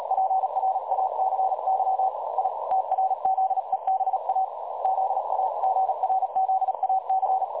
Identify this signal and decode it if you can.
Morse code, heard as a keyed tone, about 37 wpm, 760 Hz, with noise. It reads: PRACY I TRZEBA ZAGRZAC
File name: dlazaawansowanych / nagraniazpasma / websdr recording 3565.0kHz